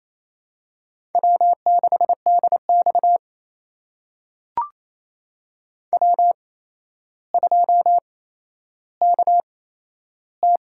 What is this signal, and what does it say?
Morse code, heard as a keyed tone, about 28 wpm, spaced as for 12 wpm, 700 Hz. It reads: W6DX W2KT
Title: USA Call Signs - 4 Characters - 8x Character Spacing - 28wpm